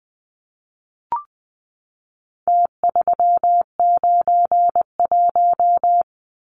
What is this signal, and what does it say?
Morse code, heard as a keyed tone, about 20 wpm, 700 Hz. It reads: T391
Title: Numbers - Contest Style - 4-Digit 20wpm